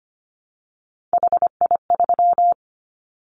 Morse code, heard as a keyed tone, about 25 wpm, 700 Hz. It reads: HI3
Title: Callsign Prefixes for Top 100 DXCC - 25wpm